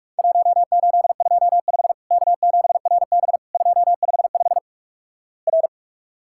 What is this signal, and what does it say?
Morse code, heard as a keyed tone, about 46 wpm, 700 Hz. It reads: 1825 K7RB 255 R
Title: Worked All Europe DX Contest - QTC Exchange - 45wpm